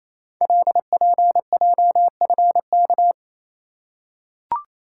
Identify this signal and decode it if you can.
Morse code, heard as a keyed tone, about 29 wpm, 700 Hz. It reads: LPJFK
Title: Code Groups - 5 Letters - 28wpm